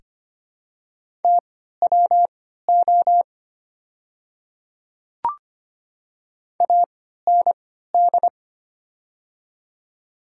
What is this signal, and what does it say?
Morse code, heard as a keyed tone, about 25 wpm, spaced as for 14 wpm, 700 Hz. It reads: TWO AND
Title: 3x Character Spacing - Top 100 Words - 25wpm